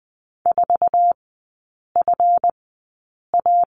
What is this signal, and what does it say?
Morse code, heard as a keyed tone, about 20 wpm, spaced as for 8 wpm, 700 Hz. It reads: 4FA
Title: 172 Lesson 28e - Number 9 - Three Characters 20wpm